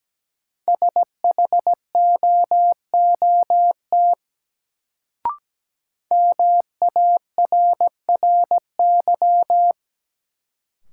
Morse code, heard as a keyed tone, about 17 wpm, 700 Hz. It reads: SHOOT MARRY
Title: Sentences from Top 1200 Words - New Word Drill Alt - 17wpm